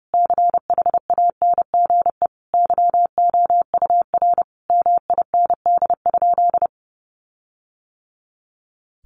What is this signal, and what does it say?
Morse code, heard as a keyed tone, about 30 wpm, 700 Hz. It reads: CHANGE YOUR MIND?
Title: Sentences from Top 1300 Words 30wpm